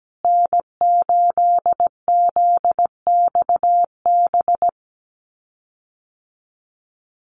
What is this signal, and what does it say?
Morse code, heard as a keyed tone, about 17 wpm, 700 Hz. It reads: N8ZXB
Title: USA Call Signs Repeated 3x 17wpm